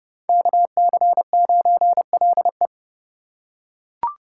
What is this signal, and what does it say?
Morse code, heard as a keyed tone, about 30 wpm, 690 Hz. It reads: KC9LE
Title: USA Call Signs - 3x Character Spacing - 30wpm